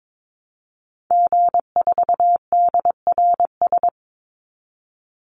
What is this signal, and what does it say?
Morse code, heard as a keyed tone, about 22 wpm, 700 Hz. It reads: G4DRS
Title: Call Signs - 22wpm